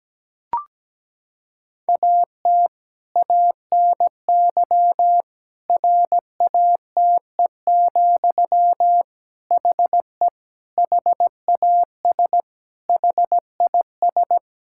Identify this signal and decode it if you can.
Morse code, heard as a keyed tone, about 17 wpm, 700 Hz. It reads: AT ANY RATE, HE HAS HIS
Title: Sentences from Top 1000 Words - Review Part 5 - 17wpm